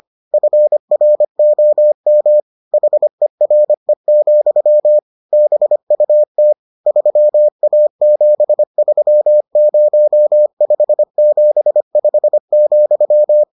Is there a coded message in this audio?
Yes: Morse code from 0.3 to 13.4 s, heard as a keyed tone, about 25 wpm, 600 Hz. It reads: FROM HERE, BUT 3A730575,